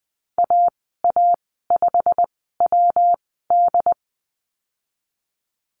Morse code, heard as a keyed tone, about 20 wpm, spaced as for 15 wpm, 700 Hz. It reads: AA5WD
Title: 122 Lesson 20g - Number 3 - Call Signs 20wpm